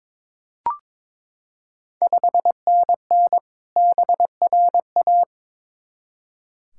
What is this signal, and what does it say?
Morse code, heard as a keyed tone, about 22 wpm, 690 Hz. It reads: 5NN BRA
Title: Ontario QSO Party Exchange 22wpm